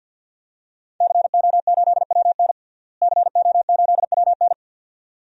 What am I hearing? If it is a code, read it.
Morse code, heard as a keyed tone, about 46 wpm, 700 Hz. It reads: CO8WN CO8WN
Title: DXCC-100 Call Signs - 5 Characters - Repeated 2x - 50wpm